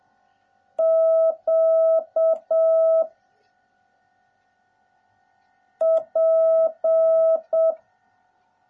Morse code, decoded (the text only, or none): Q P